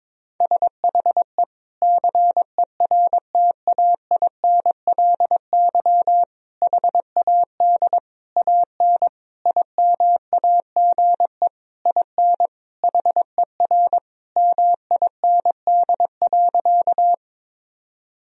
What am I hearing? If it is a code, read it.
Morse code, heard as a keyed tone, about 22 wpm, 700 Hz. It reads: SHE CERTAINLY HAD AN IMAGE IN HER MIND.
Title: Sentences from Top 1000 Words - Review Part 4 - 22wpm